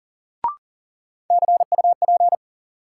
Morse code, heard as a keyed tone, about 40 wpm, 700 Hz. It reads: CUP
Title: Sentences from Top 1600 Words - New Word Drill Alt - 40wpm